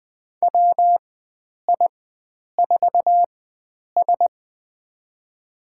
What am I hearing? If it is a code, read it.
Morse code, heard as a keyed tone, about 21 wpm, spaced as for 9 wpm, 700 Hz. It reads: WI4S
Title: USA Call Signs - 4 Characters - 4x Character Spacing - 20wpm